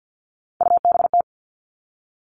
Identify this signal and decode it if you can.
Morse code, heard as a keyed tone, about 50 wpm, 700 Hz. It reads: UBT
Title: All Cut Numbers - 3-Digit 50wpm